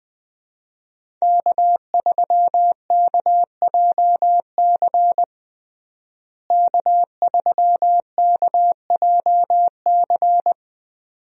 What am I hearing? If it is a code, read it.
Morse code, heard as a keyed tone, about 20 wpm, 700 Hz. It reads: K3KJC K3KJC
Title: USA Call Signs - 5 Characters - Repeated 2x - 20wpm